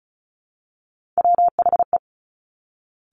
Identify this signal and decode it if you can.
Morse code, heard as a keyed tone, about 36 wpm, 700 Hz. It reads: WHE
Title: Top Three Letter Combinations in Words 35wpm